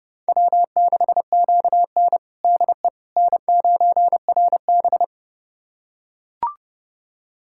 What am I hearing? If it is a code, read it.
Morse code, heard as a keyed tone, about 30 wpm, 700 Hz. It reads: W6QN DE N9RB
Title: USA Callsign DE Callsign - 4 Characters - 30wpm